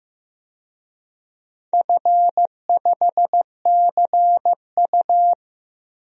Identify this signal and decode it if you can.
Morse code, heard as a keyed tone, about 15 wpm, 700 Hz. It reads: F5CU
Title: DXCC-100 Call Signs - 4 Characters - 15wpm